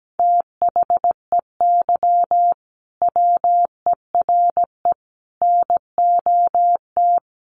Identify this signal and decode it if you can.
Morse code, heard as a keyed tone, about 17 wpm, 700 Hz. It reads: THEY WERE NOT
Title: Sentences from Top 1000 Words 17wpm